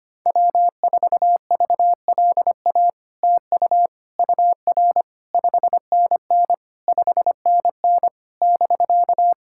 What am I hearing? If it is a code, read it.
Morse code, heard as a keyed tone, about 25 wpm, 700 Hz. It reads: W4VLA TU UR 5NN 5NN <BK>
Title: SOTA QSO Practice 25wpm